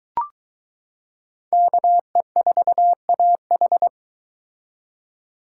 Morse code, heard as a keyed tone, about 23 wpm, 700 Hz. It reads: KE4AH